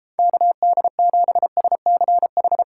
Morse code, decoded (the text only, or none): KD7SCH